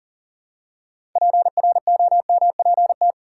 Morse code, heard as a keyed tone, about 40 wpm, 700 Hz. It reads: PROMPT
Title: Six Letter Words - 40wpm